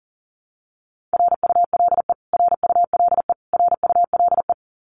Morse code, heard as a keyed tone, about 40 wpm, 700 Hz. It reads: RULE RULE RULE
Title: Top 500 Words Repeated 40wpm